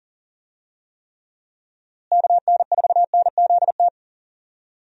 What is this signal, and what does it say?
Morse code, heard as a keyed tone, about 39 wpm, 700 Hz. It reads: KN4NZT
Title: USA Call Signs - 6 Characters - Repeated 3x - 40wpm